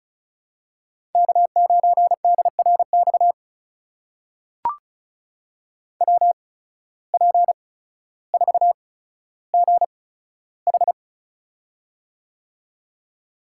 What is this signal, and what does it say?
Morse code, heard as a keyed tone, about 34 wpm, spaced as for 15 wpm, 700 Hz. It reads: K9DRX WP4GH
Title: USA Call Signs - 8x Character Spacing - 35wpm